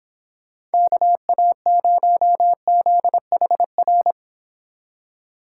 Morse code, heard as a keyed tone, about 26 wpm, 700 Hz. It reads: KA0ZHR